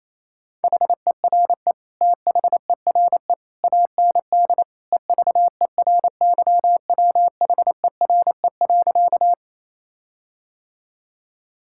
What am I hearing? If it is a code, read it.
Morse code, heard as a keyed tone, about 27 wpm, 700 Hz. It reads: HERE THERE AND EVERYWHERE.
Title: Sentences from Top 1600 Words 28wpm